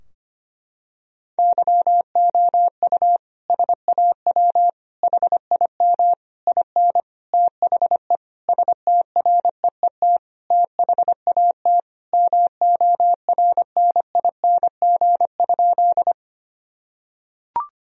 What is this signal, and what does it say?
Morse code, heard as a keyed tone, about 25 wpm, 700 Hz. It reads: YOU SAW HIM IN THE STREET THAT MORNING?
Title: Sentences from Top 1000 Words - Review Part 3 - 25wpm